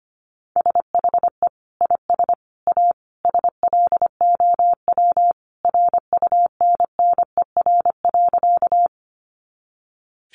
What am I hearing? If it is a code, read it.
Morse code, heard as a keyed tone, about 25 wpm, 700 Hz. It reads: SHE IS A SLOW RUNNER.
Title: Additional Easy Phrases 25wpm